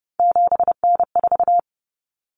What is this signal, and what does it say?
Morse code, heard as a keyed tone, about 30 wpm, 700 Hz. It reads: 7N4
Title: Common Cut Numbers - 3-Digit 30wpm